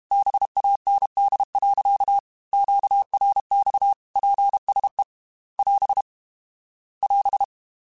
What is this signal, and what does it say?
Morse code, heard as a keyed tone, about 32 wpm, 800 Hz. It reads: BAND. QRX PSE <AS> <AS>
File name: QSOs / QSOs-mp3s / WA7QOX 2@30